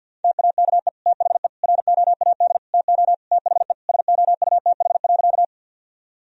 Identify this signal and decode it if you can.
Morse code, heard as a keyed tone, about 50 wpm, 700 Hz. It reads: TAKE THE ROAD TO THE SOUTH.
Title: Sentences from Top 1000 Words - Review Part 2 - 50wpm